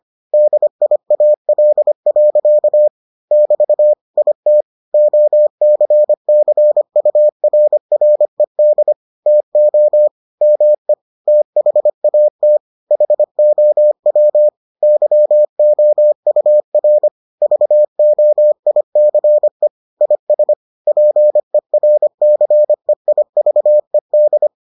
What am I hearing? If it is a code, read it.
Morse code, heard as a keyed tone, about 25 wpm, 600 Hz. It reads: DIAL. <BT> IT OCCURRED TO ME THAT HOW YOUR VOICE IS PERCEIVED